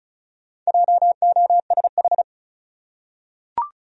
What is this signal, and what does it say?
Morse code, heard as a keyed tone, about 36 wpm, 700 Hz. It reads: JOSH
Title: USA Common Names - Max 5 Letters - 35wpm